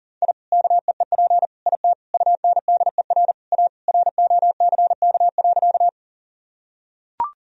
Morse code, heard as a keyed tone, about 40 wpm, 700 Hz. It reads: I KEEP IT UNDER A ROCK.
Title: Sentences from Top 1000 Words - Review Part 2 - 40wpm